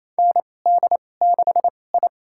Morse code, heard as a keyed tone, about 27 wpm, spaced as for 20 wpm, 700 Hz. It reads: ND6I
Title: USA Call Signs - 4 Characters - 2x Character Spacing - 28wpm